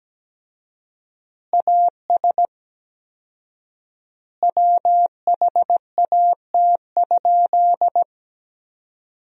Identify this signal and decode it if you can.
Morse code, heard as a keyed tone, about 17 wpm, 700 Hz. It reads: AS WHAT?
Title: 4x Word-Spacing - Sentences from Top 200 Words - 17wpm